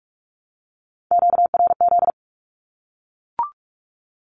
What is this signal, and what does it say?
Morse code, heard as a keyed tone, about 45 wpm, 700 Hz. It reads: QRZ
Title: Q-Codes Repeated 3x 45wpm